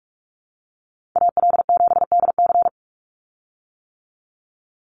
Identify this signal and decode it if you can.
Morse code, heard as a keyed tone, about 44 wpm, 700 Hz. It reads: AL7DC
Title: USA Call Signs 45wpm